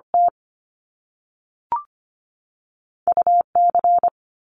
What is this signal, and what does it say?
Morse code, heard as a keyed tone, about 25 wpm, 700 Hz. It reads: T U C
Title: Top Two Letter Combinations in Words 25wpm